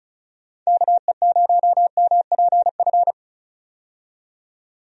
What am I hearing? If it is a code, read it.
Morse code, heard as a keyed tone, about 34 wpm, 700 Hz. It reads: KE0MPF
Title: USA Call Signs - 6 Characters - 35wpm